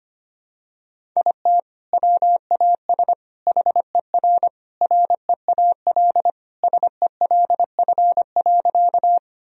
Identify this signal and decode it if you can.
Morse code, heard as a keyed tone, about 25 wpm, 700 Hz. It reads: IT WAS HER REAL SELF.
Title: Sentences from Top 1000 Words - Review Part 1 - 25wpm